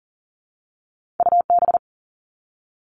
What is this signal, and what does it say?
Morse code, heard as a keyed tone, about 42 wpm, 700 Hz. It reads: UB
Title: All Cut Numbers - 2-Digit 40wpm